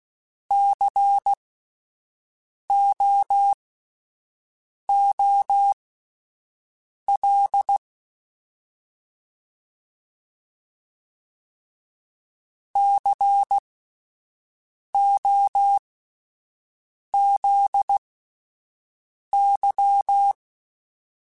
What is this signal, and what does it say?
Morse code, heard as a keyed tone, about 16 wpm, spaced as for 5 wpm, 800 Hz. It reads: COOL COZY
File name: Alphabetical / Alphabetical mp3s / Cs @5